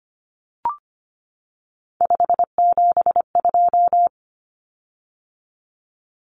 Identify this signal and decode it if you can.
Morse code, heard as a keyed tone, about 25 wpm, 700 Hz. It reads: 572
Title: Common Cut Numbers - 3-Digit 25wpm